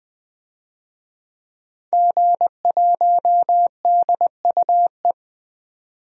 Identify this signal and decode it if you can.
Morse code, heard as a keyed tone, about 20 wpm, 700 Hz. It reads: G1DUE
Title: DXCC-100 Call Signs - 5 Characters - 20wpm